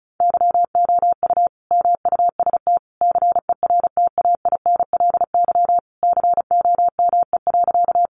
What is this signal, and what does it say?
Morse code, heard as a keyed tone, about 34 wpm, 700 Hz. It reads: YOU MUST CERTAINLY COME.
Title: Sentences from Top 900 Words 35wpm